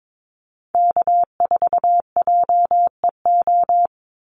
Morse code, heard as a keyed tone, about 22 wpm, 700 Hz. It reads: K4JEO
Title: USA Call Signs 22wpm